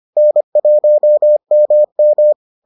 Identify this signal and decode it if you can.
Morse code, heard as a keyed tone, about 25 wpm, 600 Hz. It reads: N1MM